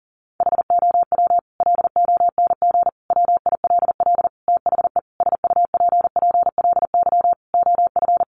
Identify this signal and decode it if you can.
Morse code, heard as a keyed tone, about 40 wpm, 700 Hz. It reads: HOW LONG WILL THE SUPPLY OF
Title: Sentences from Top 1200 Words 40wpm